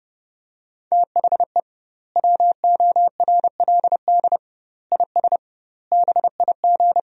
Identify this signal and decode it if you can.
Morse code, heard as a keyed tone, about 32 wpm, 700 Hz. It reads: THE WORLD IS BIG
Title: 2x Word-Spacing - Sentences from Top 400 Words - 30wpm